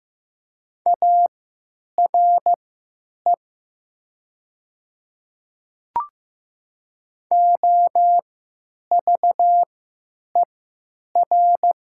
Morse code, heard as a keyed tone, about 15 wpm, spaced as for 9 wpm, 700 Hz. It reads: ARE OVER
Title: 3x Character Spacing - Top 200 Words - 15wpm